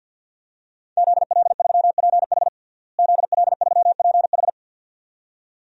Morse code, heard as a keyed tone, about 52 wpm, 700 Hz. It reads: ZL3PH ZL3PH
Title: DXCC-100 Call Signs - 5 Characters - Repeated 2x - 50wpm